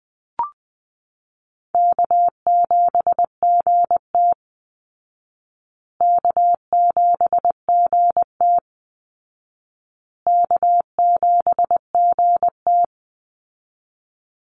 Morse code, heard as a keyed tone, about 20 wpm, 700 Hz. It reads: K7GT K7GT K7GT
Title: Easy Call Signs Repeated 3x - 20wpm